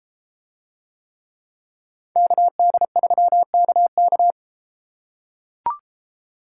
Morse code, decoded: KD3KK

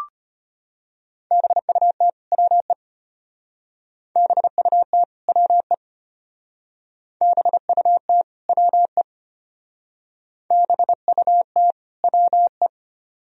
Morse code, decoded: BUT WE BUT WE BUT WE BUT WE

